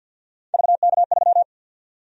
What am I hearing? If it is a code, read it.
Morse code, heard as a keyed tone, about 49 wpm, 700 Hz. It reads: VK3
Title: USA Call Sign Prefixes - 50wpm